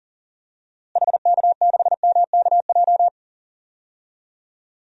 Morse code, heard as a keyed tone, about 37 wpm, 700 Hz. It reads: HK6MKJ
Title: DXCC-100 Call Signs - 6 Characters - Repeated 3x - 40wpm